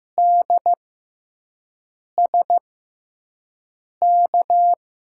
Morse code, heard as a keyed tone, about 15 wpm, spaced as for 5 wpm, 700 Hz. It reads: DSK
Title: USA Call Signs - 6 Characters - 6x Character Spacing - 15wpm